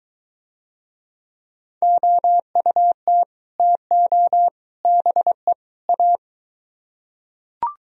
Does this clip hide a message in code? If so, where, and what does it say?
Yes: Morse code from 1.8 to 6.2 s, heard as a keyed tone, about 21 wpm, 700 Hz. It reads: OUT TO BE A